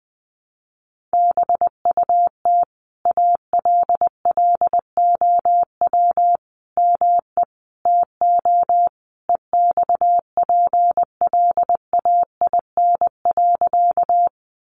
Morse code, heard as a keyed tone, about 19 wpm, 700 Hz. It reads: BUT ALLOW ME TO EXPLAIN.
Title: Sentences from Top 1100 Words 20wpm